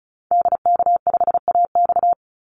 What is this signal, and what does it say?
Morse code, heard as a keyed tone, about 34 wpm, 700 Hz. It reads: DK5AX
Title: Call Signs Repeated 3x - 35wpm